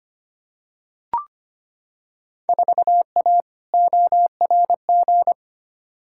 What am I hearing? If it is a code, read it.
Morse code, heard as a keyed tone, about 25 wpm, 700 Hz. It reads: 4A ORG
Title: Field Day Exchanges - 25wpm